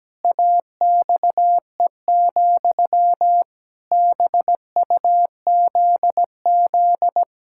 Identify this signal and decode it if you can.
Morse code, heard as a keyed tone, about 17 wpm, 700 Hz. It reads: AXE, BUZZ
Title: Pangram Challenge v2 - 17wpm